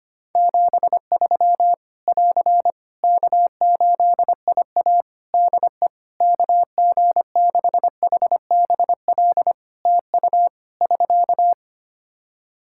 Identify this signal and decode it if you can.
Morse code, heard as a keyed tone, about 25 wpm, 700 Hz. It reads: 73 <AR> K8IA DE KG6HBL TU <SK>
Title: Practice QSO - Exchange 4 - Ending - 25wpm